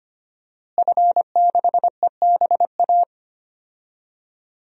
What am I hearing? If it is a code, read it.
Morse code, heard as a keyed tone, about 25 wpm, 700 Hz. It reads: F6EBA
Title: DXCC-100 Call Signs - 5 Characters - Repeated 3x - 25wpm